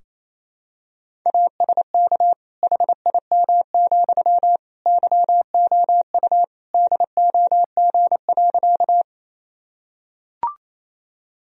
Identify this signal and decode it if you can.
Morse code, heard as a keyed tone, about 28 wpm, 700 Hz. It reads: ASK HIM, YOU DOG.
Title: Sentences from Top 1000 Words - Review Part 2 - 28wpm